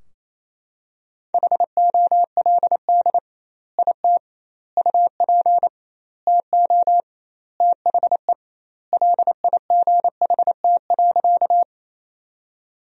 Morse code, heard as a keyed tone, about 28 wpm, 700 Hz. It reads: HOLD IT UP TO THE LIGHT.
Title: 2x Word-Spacing - Sentences from Top 400 Words - 28wpm